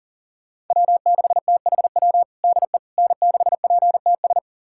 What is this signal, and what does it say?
Morse code, heard as a keyed tone, about 40 wpm, 700 Hz. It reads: W6THW DE N6PTS
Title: USA Callsign DE Callsign - 5 Characters - 40wpm